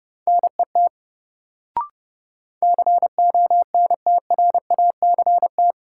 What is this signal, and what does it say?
Morse code, heard as a keyed tone, about 30 wpm, 700 Hz. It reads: NET CONTRACT
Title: Sentences from Top 1400 Words - New Word Drill - 30wpm